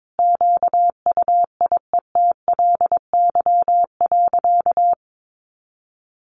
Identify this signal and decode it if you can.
Morse code, heard as a keyed tone, about 22 wpm, 700 Hz. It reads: QUIETLY.
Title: A Sound Of Thunder 22wpm